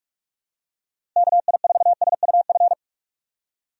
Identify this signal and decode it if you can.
Morse code, heard as a keyed tone, about 45 wpm, 700 Hz. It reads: KI4SUF
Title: USA Call Signs - 6 Characters - Repeated 2x - 45wpm